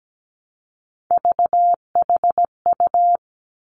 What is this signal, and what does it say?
Morse code, heard as a keyed tone, about 17 wpm, 700 Hz. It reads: VHU
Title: Three Challenging Letter-Number Combinations 17wpm